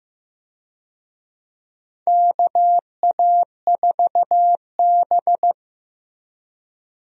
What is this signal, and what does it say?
Morse code, heard as a keyed tone, about 15 wpm, 700 Hz. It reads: KA4B
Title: USA Call Signs - 4 Characters - Repeated 3x - 15wpm